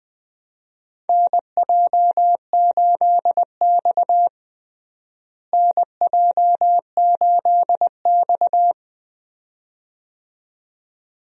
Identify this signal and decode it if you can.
Morse code, heard as a keyed tone, about 21 wpm, 700 Hz. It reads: NJ8X NJ8X
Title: USA Call Signs - 4 Characters - Repeated 2x - 20wpm